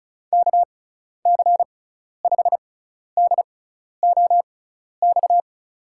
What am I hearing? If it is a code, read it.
Morse code, heard as a keyed tone, about 36 wpm, spaced as for 12 wpm, 690 Hz. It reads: KC5DOX
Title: USA Call Signs - 6 Characters - 6x Character Spacing - 35wpm